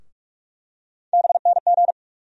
Morse code, heard as a keyed tone, about 47 wpm, 700 Hz. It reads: BNG
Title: USA Call Sign Postfixes - 3 Letters 45wpm